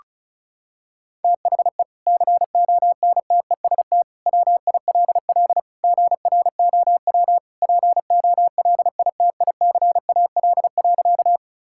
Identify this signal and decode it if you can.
Morse code, heard as a keyed tone, about 34 wpm, 700 Hz. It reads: THE CONTEST WILL GROW POLITICAL.
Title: Sentences from Top 1600 Words 35wpm